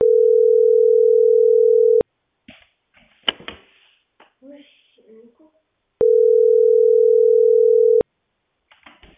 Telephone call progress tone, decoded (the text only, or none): ringback tone